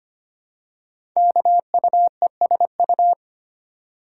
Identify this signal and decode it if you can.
Morse code, heard as a keyed tone, about 25 wpm, 700 Hz. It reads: KUESU
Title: Code Groups - 5 Letter:Numbers - 25wpm